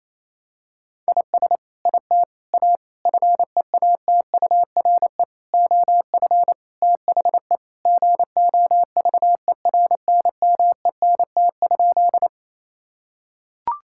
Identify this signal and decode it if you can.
Morse code, heard as a keyed tone, about 28 wpm, 700 Hz. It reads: IS IT A FEATURE OF THE GOVERNMENT?
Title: Sentences from Top 1000 Words - Review Part 5 - 28wpm